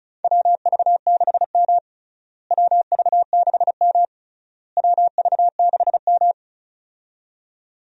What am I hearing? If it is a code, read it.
Morse code, heard as a keyed tone, about 34 wpm, 700 Hz. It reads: WV6M WV6M WV6M
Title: USA Call Signs - 4 Characters - Repeated 3x - 35wpm